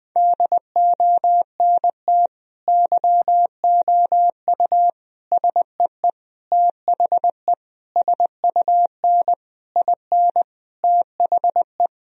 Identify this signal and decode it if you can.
Morse code, heard as a keyed tone, about 20 wpm, 700 Hz. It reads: DONT YOU SEE THE SUN IN THE
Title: Sentences from Top 1000 Words - Review Part 2 - 20wpm